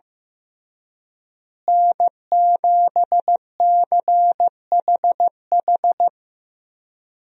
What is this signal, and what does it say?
Morse code, heard as a keyed tone, about 15 wpm, 700 Hz. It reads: N7CHH
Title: USA Call Signs - 5 Characters - Repeated 2x - 15wpm